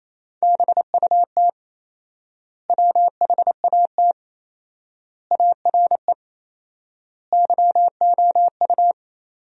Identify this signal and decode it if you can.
Morse code, heard as a keyed tone, about 28 wpm, 700 Hz. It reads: BUT WHAT ARE YOU
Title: 4x Word-Spacing - Sentences from Top 100 Words - 28wpm